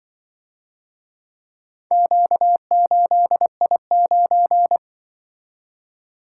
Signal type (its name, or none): Morse code